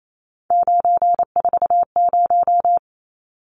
Morse code, heard as a keyed tone, about 28 wpm, 700 Hz. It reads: 940
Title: Numbers - 3-Digit 28wpm